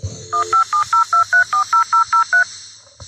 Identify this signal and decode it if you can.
DTMF, heard as telephone touch tones, tones of about 120 ms. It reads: 1702340003